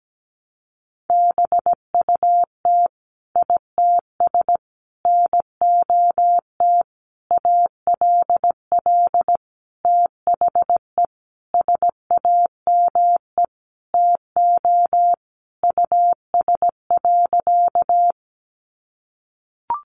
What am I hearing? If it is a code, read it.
Morse code, heard as a keyed tone, about 17 wpm, 700 Hz. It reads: BUT ITS NOT ALL THE SAME TO US.
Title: Sentences from Top 600 Words 17wpm